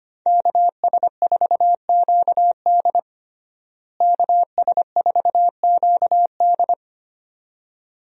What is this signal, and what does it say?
Morse code, heard as a keyed tone, about 25 wpm, 700 Hz. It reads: KS4QD KS4QD
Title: USA Call Signs - 5 Characters - Repeated 2x - 25wpm